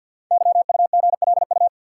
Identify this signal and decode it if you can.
Morse code, heard as a keyed tone, about 50 wpm, 700 Hz. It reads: YUGLU
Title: Code Groups - 5 Letter:Numbers - 50wpm